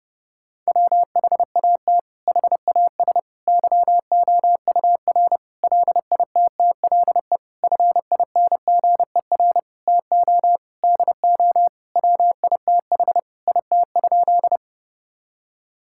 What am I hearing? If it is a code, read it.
Morse code, heard as a keyed tone, about 32 wpm, 700 Hz. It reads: WHAT HAS YOUR LITTLE FINGER TO DO WITH IT?
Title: Sentences from Top 1600 Words 30wpm